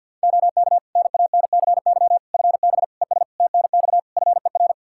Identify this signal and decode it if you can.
Morse code, heard as a keyed tone, about 52 wpm, 700 Hz. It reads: OK NANCY FB ES TNX FER
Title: Practice QSO - Exchange 2 - Info - 50wpm